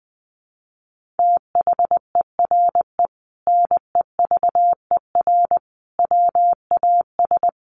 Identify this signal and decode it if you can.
Morse code, heard as a keyed tone, about 20 wpm, 700 Hz. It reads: THERE NEVER WAS
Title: Sentences from Top 1500 Words 20wpm